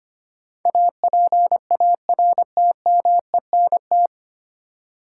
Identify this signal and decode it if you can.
Morse code, heard as a keyed tone, about 25 wpm, 690 Hz. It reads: APARTMENT
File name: practice / 25wpm / 9 Letter Words - First 3 Letters Given - 25wpm